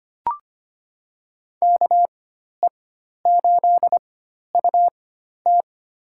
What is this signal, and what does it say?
Morse code, heard as a keyed tone, about 25 wpm, spaced as for 12 wpm, 700 Hz. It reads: KE8UT
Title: USA Call Signs - 4x Character Spacing - 25wpm